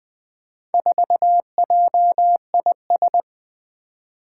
Morse code, heard as a keyed tone, about 20 wpm, 700 Hz. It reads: 4JIS
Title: Four Challenging Letter-Number Combinations 20wpm